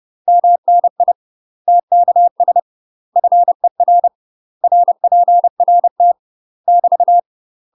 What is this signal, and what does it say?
Morse code, heard as a keyed tone, about 30 wpm, 700 Hz. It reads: MNI TKS FER RPRT <BT>
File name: UK MASTER QSOs TRAINING / 30WPM / UK MASTER QSOs 0003